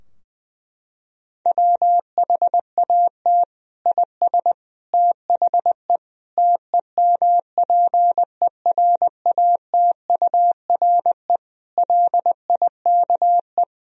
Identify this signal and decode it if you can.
Morse code, heard as a keyed tone, about 20 wpm, 700 Hz. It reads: WHAT IS THE TEMPERATURE LIKE
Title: Sentences from Top 1000 Words - Review Part 4 - 20wpm